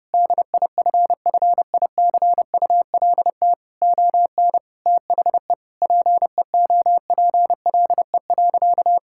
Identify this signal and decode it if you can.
Morse code, heard as a keyed tone, about 30 wpm, 700 Hz. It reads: DIFFICULT ON THE PEOPLE.
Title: Sentences from Top 1000 Words - Review Part 5 - 30wpm